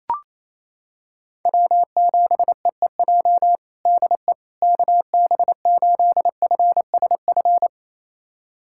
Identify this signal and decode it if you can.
Morse code, heard as a keyed tone, about 28 wpm, 700 Hz. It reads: W7EEJ DE KB8FSF
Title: USA Callsign DE Callsign 28wpm